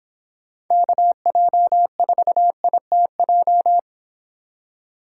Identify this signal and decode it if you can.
Morse code, heard as a keyed tone, about 26 wpm, 700 Hz. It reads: KJ4ITJ